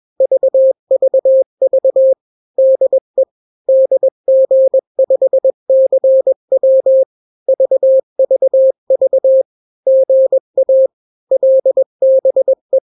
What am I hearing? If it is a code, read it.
Morse code, heard as a keyed tone, about 21 wpm, 530 Hz. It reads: VVV DE DG5CW VVV GA LBE
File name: daily / 20wpm